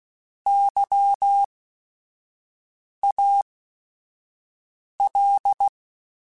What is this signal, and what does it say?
Morse code, heard as a keyed tone, about 16 wpm, spaced as for 5 wpm, 800 Hz. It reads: YAL